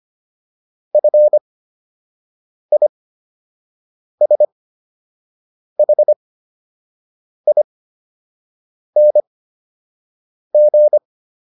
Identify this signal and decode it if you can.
Morse code, heard as a keyed tone, about 25 wpm, spaced as for 6 wpm, 600 Hz. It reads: FISHING